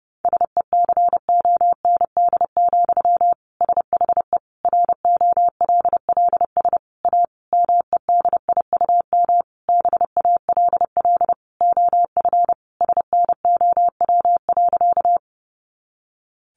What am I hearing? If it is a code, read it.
Morse code, heard as a keyed tone, about 28 wpm, 700 Hz. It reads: SECOND, SHE ROLLS A MEDIUM BALL OF SNOW.
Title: Easy Phrases 30wpm